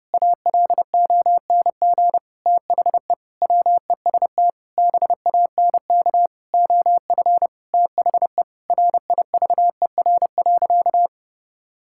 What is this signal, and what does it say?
Morse code, heard as a keyed tone, about 30 wpm, 700 Hz. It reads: ALONG THE WEST BANK OF THE RIVER.
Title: Sentences from Top 1000 Words - Review Part 6 - 30wpm